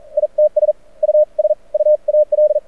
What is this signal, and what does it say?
Morse code, heard as a keyed tone, about 47 wpm, spaced as for 36 wpm, 590 Hz. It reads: ITS USUAL
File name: archived / CRSnet